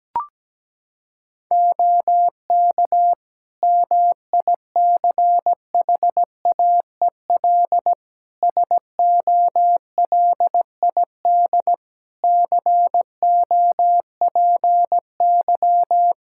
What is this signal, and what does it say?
Morse code, heard as a keyed tone, about 17 wpm, 700 Hz. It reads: OK MICHAEL SOLID COPY
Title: K1USN SST QSO Practice 17wpm